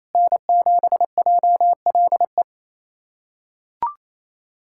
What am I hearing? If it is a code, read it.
Morse code, heard as a keyed tone, about 28 wpm, 700 Hz. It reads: N7JLE
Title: USA Callsign DE Callsign - 5 Characters - 28wpm